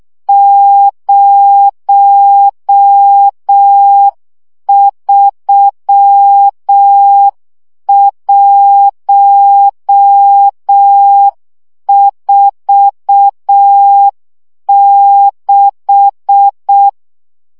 Morse code, heard as a keyed tone, about 6 wpm, 800 Hz. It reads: 03146
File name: morze vizsga 2011-3 5